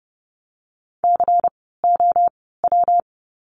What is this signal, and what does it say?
Morse code, heard as a keyed tone, about 30 wpm, spaced as for 17 wpm, 700 Hz. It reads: COW
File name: course / 30:10wpm / 115 Lesson 19f - Letter w - Words 30wpm